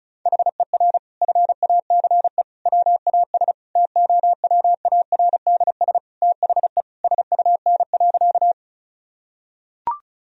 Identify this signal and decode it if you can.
Morse code, heard as a keyed tone, about 36 wpm, 700 Hz. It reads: HER FACE WAS TOWARDS THE SUN.
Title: Sentences from Top 1000 Words - Review Part 2 - 35wpm